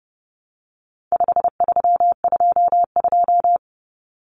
Morse code, heard as a keyed tone, about 29 wpm, 700 Hz. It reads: 5322
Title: Common Cut Numbers - 4-Digit 30wpm